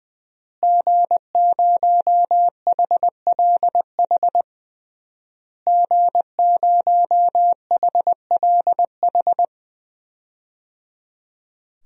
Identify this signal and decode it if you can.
Morse code, heard as a keyed tone, about 21 wpm, 700 Hz. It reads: G0HLH G0HLH
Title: DXCC-100 Call Signs - 5 Characters - Repeated 3x - 20wpm